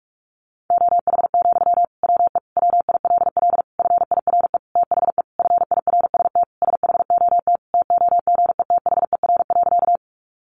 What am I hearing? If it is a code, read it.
Morse code, heard as a keyed tone, about 44 wpm, 700 Hz. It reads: OH, WE WILL FIRE THE FIRST SHOT TOGETHER.
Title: Sentences from Top 1100 Words 45wpm